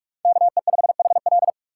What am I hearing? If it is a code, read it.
Morse code, heard as a keyed tone, about 41 wpm, 700 Hz. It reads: KE5HL